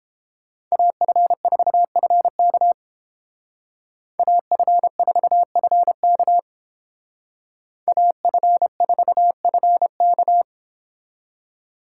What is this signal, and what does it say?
Morse code, heard as a keyed tone, about 28 wpm, 700 Hz. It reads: AF4FK AF4FK AF4FK